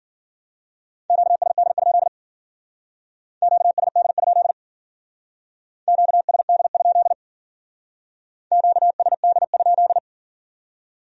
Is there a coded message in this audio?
Yes: Morse code from 1.1 to 10.0 s, heard as a keyed tone, about 53 wpm, 700 Hz. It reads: QSD? QSD? QSD? QSD?